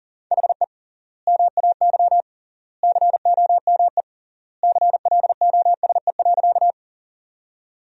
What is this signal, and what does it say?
Morse code, heard as a keyed tone, about 40 wpm, 700 Hz. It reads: HE MAY COME CLOSE.
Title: 3x Word-Spacing - Sentences from Top 300 Words - 40wpm